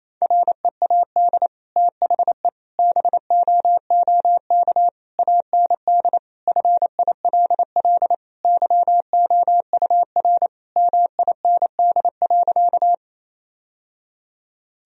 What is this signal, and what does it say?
Morse code, heard as a keyed tone, about 28 wpm, 700 Hz. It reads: READ THE BOOK AND FILL YOUR MIND.
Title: Sentences from Top 1000 Words - Review Part 3 - 28wpm